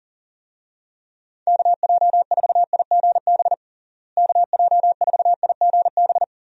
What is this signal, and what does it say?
Morse code, heard as a keyed tone, about 40 wpm, 700 Hz. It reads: KJ4IGB KJ4IGB
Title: USA Call Signs - 6 Characters - Repeated 3x - 40wpm